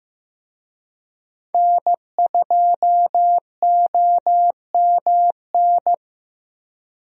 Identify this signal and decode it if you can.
Morse code, heard as a keyed tone, about 15 wpm, 700 Hz. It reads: N2OMN
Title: USA Call Signs - 5 Characters - Repeated 3x - 15wpm